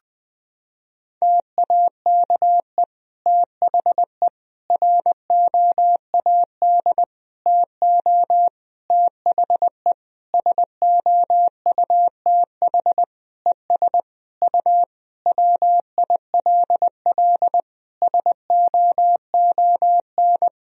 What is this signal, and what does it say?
Morse code, heard as a keyed tone, about 20 wpm, 700 Hz. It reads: TAKE THE ROAD TO THE SOUTH ES U WILL SOON
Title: Sentences from Top 400 Words and Abbreviations 20wpm